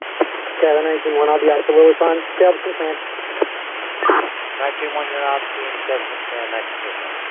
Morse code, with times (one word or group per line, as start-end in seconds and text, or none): none